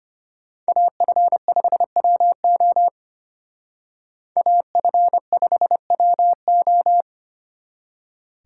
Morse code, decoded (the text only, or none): AF5WO AF5WO